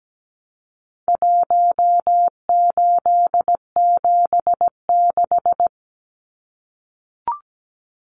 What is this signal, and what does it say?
Morse code, heard as a keyed tone, about 17 wpm, 700 Hz. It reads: 1876
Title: Numbers - Contest Style - 4-Digit 17wpm